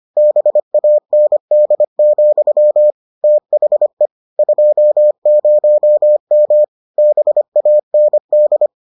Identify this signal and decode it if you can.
Morse code, heard as a keyed tone, about 25 wpm, 600 Hz. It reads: BAND, THE 20M BAND